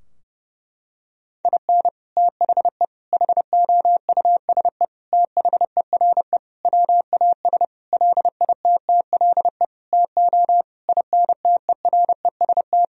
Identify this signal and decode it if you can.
Morse code, heard as a keyed tone, about 32 wpm, 700 Hz. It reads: IN THE HOUSE THERE WAS LITTLE TO INTEREST
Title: Sentences from Top 1000 Words - Review Part 2 - 30wpm